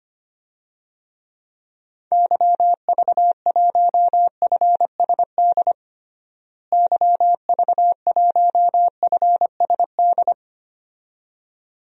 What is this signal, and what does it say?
Morse code, heard as a keyed tone, about 25 wpm, 700 Hz. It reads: YV1FSD YV1FSD
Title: DXCC-100 Call Signs - 6 Characters - Repeated 2x - 25wpm